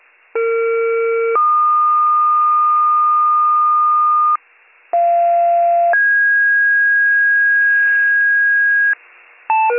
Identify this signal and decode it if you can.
Two-tone page, heard as a one-second tone, then a three-second tone, about 460 then 1220 Hz, about 690 then 1650 Hz, with noise. It is heard from 0.3 to 8.9 s.